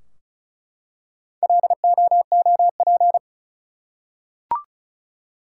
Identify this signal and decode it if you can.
Morse code, heard as a keyed tone, about 34 wpm, 700 Hz. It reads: LOOP
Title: Top 180 QSO Elements 35wpm